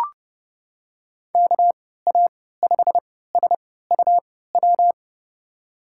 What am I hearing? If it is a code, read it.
Morse code, heard as a keyed tone, about 31 wpm, spaced as for 17 wpm, 700 Hz. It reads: KA5SUW
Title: USA Call Signs - 3x Character Spacing - 30wpm